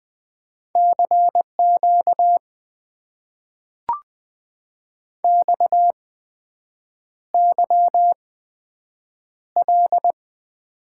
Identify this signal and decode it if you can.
Morse code, heard as a keyed tone, about 20 wpm, spaced as for 7 wpm, 700 Hz. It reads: CQ XYL
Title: Top 90 QSO Elements - 8x Character Spacing - 20wpm